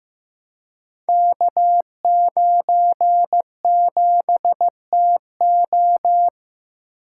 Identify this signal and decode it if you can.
Morse code, heard as a keyed tone, about 15 wpm, 700 Hz. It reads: K97TO
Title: Code Groups - 5 Letter:Numbers - 15wpm